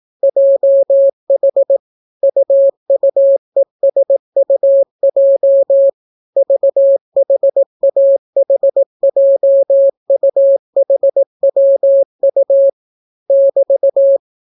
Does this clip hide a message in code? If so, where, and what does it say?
Yes: Morse code from 0.2 to 14.2 s, heard as a keyed tone, about 18 wpm, 550 Hz. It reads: JH UUESUJ VHAHJUHWU <BT>